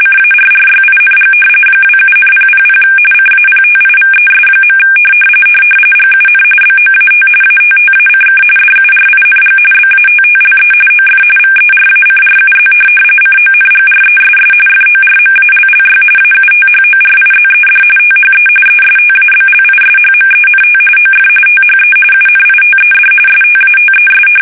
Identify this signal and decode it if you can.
RTTY, about 45.45 baud. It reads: CARRYING 27,000 CAPTIVES. GUN BOAT ATTACK DURING BATTLE OF VERACRUZ CONGRESS FORMALLY AUTHORIZED THE ESTABLISHMENT OF THE UNITED STATES
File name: fsk850 / usnavy2